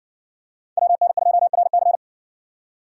Morse code, heard as a keyed tone, about 60 wpm, 700 Hz. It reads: WN2RC